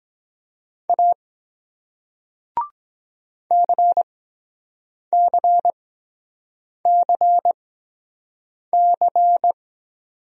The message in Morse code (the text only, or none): A CCCC